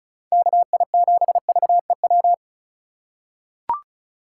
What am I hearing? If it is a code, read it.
Morse code, heard as a keyed tone, about 36 wpm, 700 Hz. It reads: KI7VEW
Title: USA Call Signs - 6 Characters - 4x Character Spacing - 35wpm